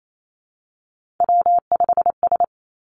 Morse code, heard as a keyed tone, about 28 wpm, 700 Hz. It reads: W5S
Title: Three Challenging Letter-Number Combinations 28wpm